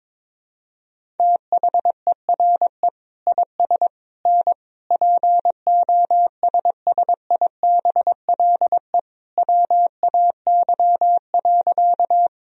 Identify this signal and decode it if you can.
Morse code, heard as a keyed tone, about 22 wpm, 700 Hz. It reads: THERE IS N POSSIBLE WAY.
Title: Sentences from Top 500 Words and Abbreviations 22wpm